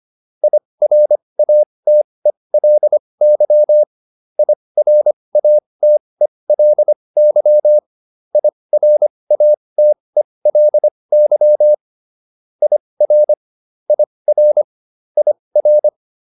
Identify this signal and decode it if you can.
Morse code, heard as a keyed tone, about 25 wpm, spaced as for 20 wpm, 600 Hz. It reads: IRATELY IRATELY IRATELY IR IR IR